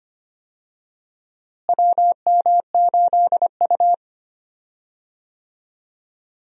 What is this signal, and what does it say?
Morse code, heard as a keyed tone, about 25 wpm, 700 Hz. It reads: WM8U
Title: USA Call Signs - 4 Characters - Repeated 3x - 25wpm